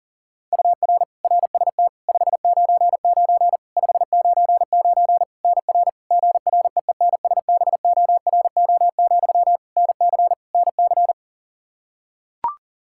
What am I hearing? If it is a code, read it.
Morse code, heard as a keyed tone, about 40 wpm, 700 Hz. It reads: UR RST 599 599 NR GREENSBORO, NC NC
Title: Practice QSO - Exchange 1 - Beginning - 40wpm